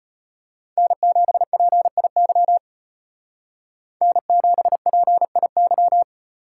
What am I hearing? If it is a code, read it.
Morse code, heard as a keyed tone, about 30 wpm, 700 Hz. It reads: N7PIY N7PIY
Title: Speed Racing - USA 1x3 Call Signs 25wpm